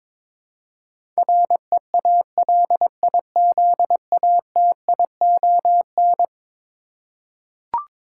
Morse code, heard as a keyed tone, about 21 wpm, 700 Hz. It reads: REALIZATION
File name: practice / 22wpm / Eleven Letter Words - 22wpm